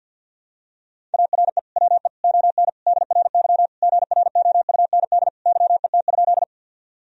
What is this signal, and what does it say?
Morse code, heard as a keyed tone, about 51 wpm, 700 Hz. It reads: ARE WE ON DRY GROUND YET?
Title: Sentences from Top 1000 Words - Review Part 3 - 50wpm